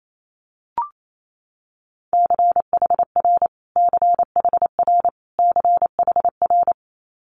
Morse code, heard as a keyed tone, about 28 wpm, 700 Hz. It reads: CHR CHR CHR
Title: Multi-letter Phonemes Repeated 3x 28wpm